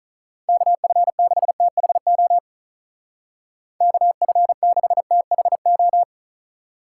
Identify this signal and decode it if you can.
Morse code, heard as a keyed tone, about 44 wpm, 700 Hz. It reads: KF6THO KF6THO